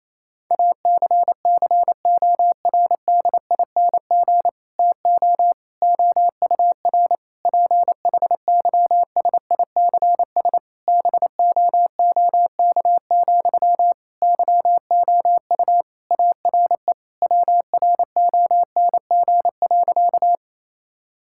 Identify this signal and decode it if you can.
Morse code, heard as a keyed tone, about 28 wpm, 700 Hz. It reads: ACCORDING TO OUR PHYSICS BOOK, YOU ARE WRONG.
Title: Sentences from Top 1000 Words - Review Part 5 - 28wpm